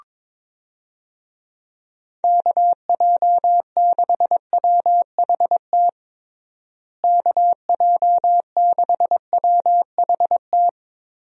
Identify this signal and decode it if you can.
Morse code, heard as a keyed tone, about 22 wpm, 700 Hz. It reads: KJ6WHT KJ6WHT
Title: USA Call Signs - 6 Characters - Repeated 3x - 22wpm